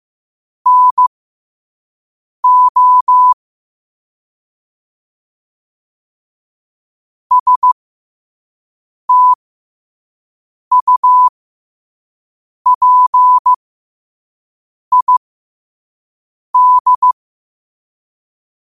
Morse code, heard as a keyed tone, about 15 wpm, spaced as for 5 wpm, 1000 Hz. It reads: NO STUPID
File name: qotdfiles / 5wpm